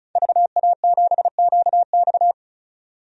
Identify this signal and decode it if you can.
Morse code, heard as a keyed tone, about 36 wpm, 700 Hz. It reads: VA7QX
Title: DXCC-100 Call Signs - 5 Characters - 35wpm